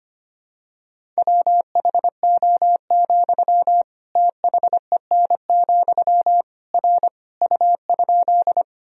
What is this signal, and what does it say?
Morse code, heard as a keyed tone, about 25 wpm, 700 Hz. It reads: WHO, THEN, R U?
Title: Sentences from Top 100 Words and Abbreviations 25wpm